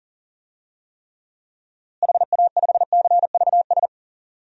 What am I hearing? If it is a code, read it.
Morse code, heard as a keyed tone, about 40 wpm, 700 Hz. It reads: HA5CVS
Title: DXCC-100 Call Signs - 6 Characters - Repeated 2x - 40wpm